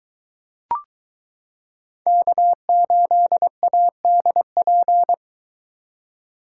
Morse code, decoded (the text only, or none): K8ADP